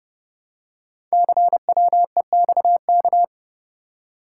Morse code, heard as a keyed tone, about 30 wpm, 700 Hz. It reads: CWEXK
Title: Code Groups - 5 Letters - 30wpm